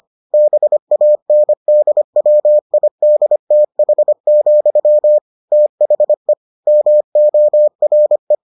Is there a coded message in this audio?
Yes: Morse code from 0.3 to 8.4 s, heard as a keyed tone, about 25 wpm, 600 Hz. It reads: BANDWIDTH, THE MORE